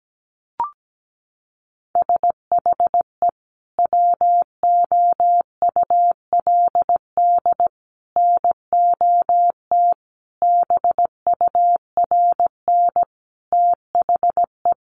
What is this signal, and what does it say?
Morse code, heard as a keyed tone, about 17 wpm, 700 Hz. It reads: SHE WOULD NOT BURN THE
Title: Sentences from Top 1300 Words 17wpm